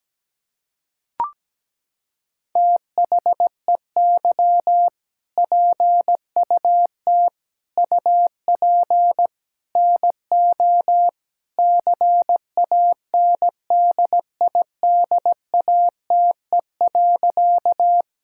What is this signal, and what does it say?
Morse code, heard as a keyed tone, about 17 wpm, 700 Hz. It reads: THEY PUT UP NO CANDIDATE.
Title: Sentences from Top 1600 Words 17wpm